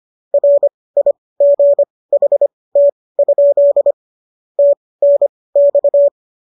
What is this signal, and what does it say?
Morse code, heard as a keyed tone, about 25 wpm, spaced as for 18 wpm, 570 Hz. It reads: RIGHT? TNX